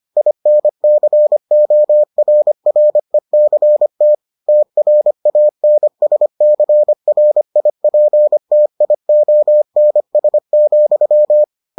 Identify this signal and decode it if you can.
Morse code, heard as a keyed tone, about 25 wpm, 600 Hz. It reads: INCORRECT TRANSCRIPTIONS,